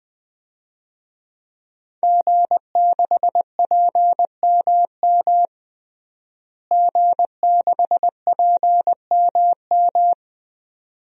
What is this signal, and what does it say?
Morse code, heard as a keyed tone, about 20 wpm, 700 Hz. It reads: G6PMM G6PMM
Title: DXCC-100 Call Signs - 5 Characters - Repeated 2x - 20wpm